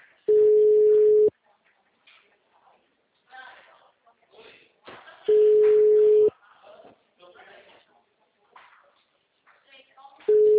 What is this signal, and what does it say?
Ringback tone (Europe, 425 Hz), with speech.
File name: out- -2002- - - .2